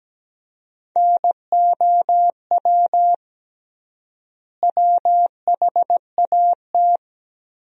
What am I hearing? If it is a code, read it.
Morse code, heard as a keyed tone, about 17 wpm, 700 Hz. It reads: NOW WHAT
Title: 3x Word-Spacing - Sentences from Top 100 Words - 17wpm